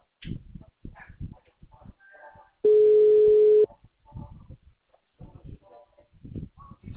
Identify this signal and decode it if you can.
Ringback tone (Europe, 425 Hz).
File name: out- -2014- - - .73692